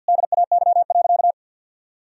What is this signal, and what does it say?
Morse code, heard as a keyed tone, about 50 wpm, 700 Hz. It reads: DAY.
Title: Sentences from Top 1000 Words - Review Part 4 - 50wpm